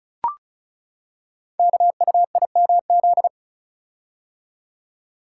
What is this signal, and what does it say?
Morse code, heard as a keyed tone, about 36 wpm, 700 Hz. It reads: KUIMZ
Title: Code Groups - 5 Letters - 35wpm